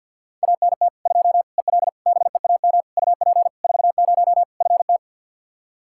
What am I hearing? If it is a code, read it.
Morse code, heard as a keyed tone, about 50 wpm, 700 Hz. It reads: ANT 2 EL BEAM UP 40 FT
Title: Practice QSO - Exchange 2 - Info - 50wpm